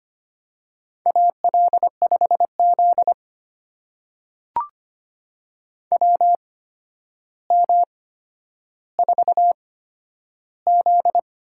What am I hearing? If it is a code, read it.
Morse code, heard as a keyed tone, about 25 wpm, spaced as for 11 wpm, 700 Hz. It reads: AL5Z WM4Z
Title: USA Call Signs - 4 Characters - 8x Character Spacing - 25wpm